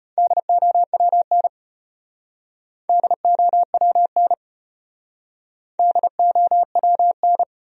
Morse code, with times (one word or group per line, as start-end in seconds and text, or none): 0.2-1.5 DOWN
2.9-4.3 DOWN
5.8-7.4 DOWN